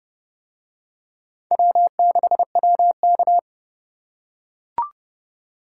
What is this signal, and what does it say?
Morse code, heard as a keyed tone, about 29 wpm, 700 Hz. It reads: W6WK